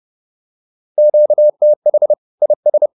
Morse code, heard as a keyed tone, about 30 wpm, 600 Hz. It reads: QTH IS